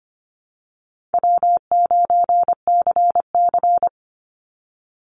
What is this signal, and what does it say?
Morse code, heard as a keyed tone, about 25 wpm, 700 Hz. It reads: W9CC
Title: Call Signs Repeated 2x - 25wpm